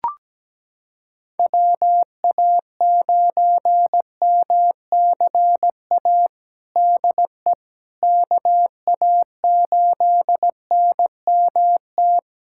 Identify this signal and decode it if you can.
Morse code, heard as a keyed tone, about 17 wpm, 700 Hz. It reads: WA9MCA DE KA8NMT
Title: USA Callsign DE Callsign - 6 Characters - 17wpm